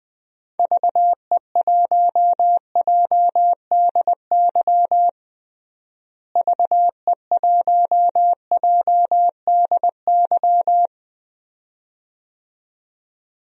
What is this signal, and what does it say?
Morse code, heard as a keyed tone, about 20 wpm, 700 Hz. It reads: VE1JDY VE1JDY
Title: DXCC-100 Call Signs - 6 Characters - Repeated 2x - 20wpm